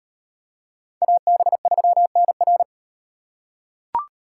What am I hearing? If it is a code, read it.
Morse code, heard as a keyed tone, about 35 wpm, 700 Hz. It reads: AB3NR